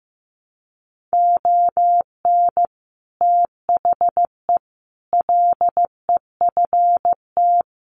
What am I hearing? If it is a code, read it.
Morse code, heard as a keyed tone, about 15 wpm, 700 Hz. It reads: ON THE LEFT
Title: Sentences from Top 700 Words 15wpm